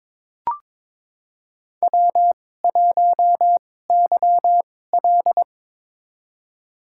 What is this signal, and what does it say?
Morse code, heard as a keyed tone, about 22 wpm, spaced as for 16 wpm, 700 Hz. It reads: W1YL
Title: USA Call Signs - 4 Characters - 2x Character Spacing - 22wpm